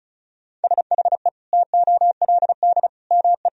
Morse code, heard as a keyed tone, about 36 wpm, 700 Hz. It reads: SHE TOLD ME
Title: Sentences from Top 400 Words and Abbreviations 35wpm